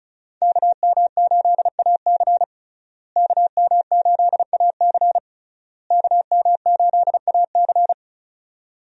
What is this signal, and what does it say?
Morse code, heard as a keyed tone, about 36 wpm, 700 Hz. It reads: KM8AC KM8AC KM8AC
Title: USA Call Signs - 5 Characters - Repeated 3x - 35wpm